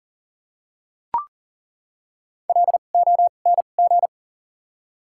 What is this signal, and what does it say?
Morse code, heard as a keyed tone, about 40 wpm, 700 Hz. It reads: L O N G
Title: 2x Character Spacing - Top 100 Words - 40wpm